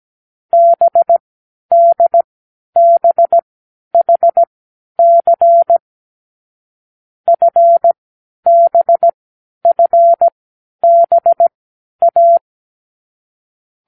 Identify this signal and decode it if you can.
Morse code, heard as a keyed tone, about 17 wpm, spaced as for 10 wpm, 680 Hz. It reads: BDBHC FBFBA
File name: k7qo / Lesson12(A-H)